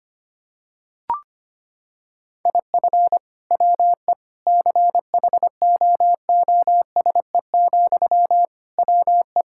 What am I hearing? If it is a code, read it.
Morse code, heard as a keyed tone, about 25 wpm, 700 Hz. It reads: IF WE CHOOSE, WE
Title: Sentences from Top 1000 Words - Review Part 6 - 25wpm